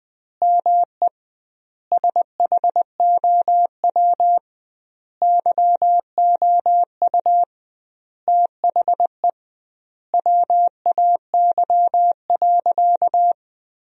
Morse code, heard as a keyed tone, about 20 wpm, 700 Hz. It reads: ME SHOW YOU THE WAY.
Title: 2x Word-Spacing - Sentences from Top 300 Words - 20wpm